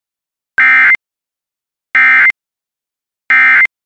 Emergency Alert System (SAME) tones, heard as harsh buzzing bursts. It is heard from 0.6 to 3.7 s.